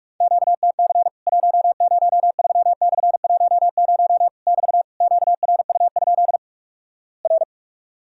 Morse code, heard as a keyed tone, about 44 wpm, 700 Hz. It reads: QTC 103/10 <BT> QRU? R
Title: Worked All Europe DX Contest - QTC Exchange - 45wpm